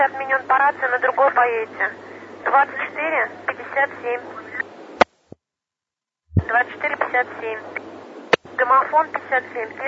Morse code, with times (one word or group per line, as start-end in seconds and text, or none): none